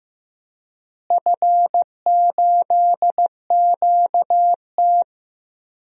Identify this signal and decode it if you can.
Morse code, heard as a keyed tone, about 16 wpm, 700 Hz. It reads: F8QT
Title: DXCC-100 Call Signs - 4 Characters - Repeated 3x - 15wpm